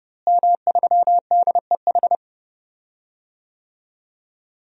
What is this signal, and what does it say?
Morse code, heard as a keyed tone, about 30 wpm, 700 Hz. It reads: M3DEH
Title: DXCC-100 Call Signs - 5 Characters - Repeated 2x - 30wpm